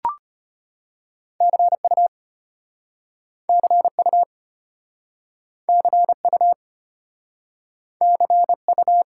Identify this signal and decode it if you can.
Morse code, heard as a keyed tone, about 34 wpm, 700 Hz. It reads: CU CU CU CU